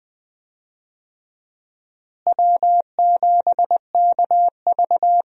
Morse code, heard as a keyed tone, about 20 wpm, 700 Hz. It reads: W7KV
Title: USA Call Signs - 4 Characters - Repeated 2x - 20wpm